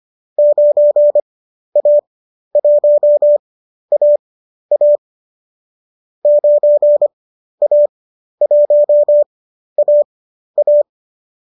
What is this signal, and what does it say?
Morse code, heard as a keyed tone, about 25 wpm, spaced as for 12 wpm, 590 Hz. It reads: 9A1AA 9A1AA
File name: qso012 12